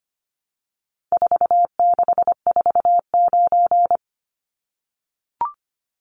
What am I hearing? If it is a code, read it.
Morse code, heard as a keyed tone, about 25 wpm, 700 Hz. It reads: 4649
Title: Numbers - 4-Digit 25wpm